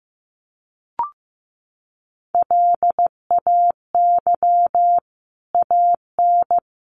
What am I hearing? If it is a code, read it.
Morse code, heard as a keyed tone, about 15 wpm, 700 Hz. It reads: LAY AN
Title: A Sound Of Thunder 15wpm